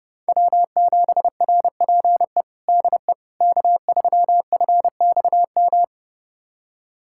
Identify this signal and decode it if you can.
Morse code, heard as a keyed tone, about 30 wpm, 700 Hz. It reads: W7RPE DE K3FXM
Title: USA Callsign DE Callsign - 5 Characters - 30wpm